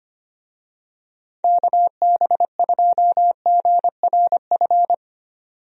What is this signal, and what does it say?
Morse code, heard as a keyed tone, about 25 wpm, 700 Hz. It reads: KB2GRF
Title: USA Call Signs - 6 Characters - 25wpm